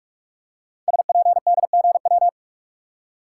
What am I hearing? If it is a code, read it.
Morse code, heard as a keyed tone, about 44 wpm, 700 Hz. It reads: SPDGW
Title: Code Groups - 5 Letters - 45wpm